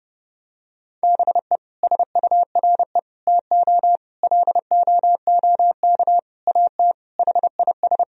Morse code, heard as a keyed tone, about 32 wpm, 700 Hz. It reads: BE SURE TO LOOK AT HIS
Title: Sentences from Top 1000 Words - Review Part 4 - 30wpm